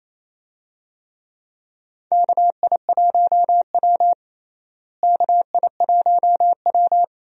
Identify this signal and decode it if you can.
Morse code, heard as a keyed tone, about 28 wpm, 700 Hz. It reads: KI1W KI1W
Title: USA Call Signs - 4 Characters - Repeated 3x - 28wpm